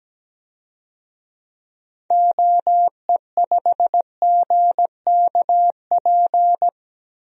Morse code, heard as a keyed tone, about 17 wpm, 700 Hz. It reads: OE5GKP
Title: DXCC-100 Call Signs - 6 Characters - 17wpm